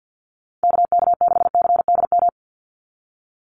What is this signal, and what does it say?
Morse code, heard as a keyed tone, about 50 wpm, 700 Hz. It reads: KK6CDM
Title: USA Call Signs 50wpm